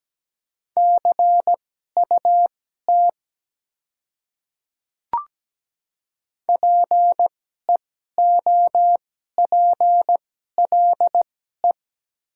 Morse code, heard as a keyed tone, about 17 wpm, spaced as for 12 wpm, 700 Hz. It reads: CUT PEOPLE
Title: 2x Character Spacing - Top 300 Words - 17wpm